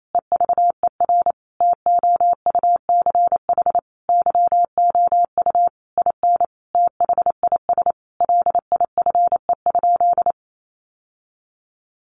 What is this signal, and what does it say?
Morse code, heard as a keyed tone, about 28 wpm, 700 Hz. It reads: EVER TOUCH YOU IN THIS LIFE?
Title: Sentences from Top 1200 Words 28wpm